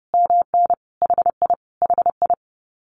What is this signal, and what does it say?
Morse code, heard as a keyed tone, about 30 wpm, 700 Hz. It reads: MN HI HI
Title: Common QSO Phrases Encore 30wpm